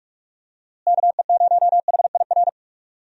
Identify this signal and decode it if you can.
Morse code, heard as a keyed tone, about 46 wpm, 700 Hz. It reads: KE0HIR
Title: USA Call Signs - 6 Characters - Repeated 3x - 45wpm